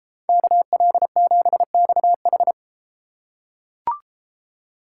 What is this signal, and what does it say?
Morse code, heard as a keyed tone, about 32 wpm, 700 Hz. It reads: KL7XH